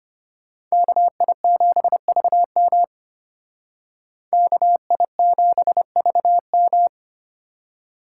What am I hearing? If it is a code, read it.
Morse code, heard as a keyed tone, about 33 wpm, 700 Hz. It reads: KI7VM KI7VM